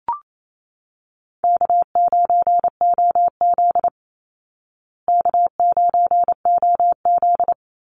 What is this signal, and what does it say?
Morse code, heard as a keyed tone, about 28 wpm, 700 Hz. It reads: K9OZ K9OZ
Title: Call Signs Repeated 3x - 28wpm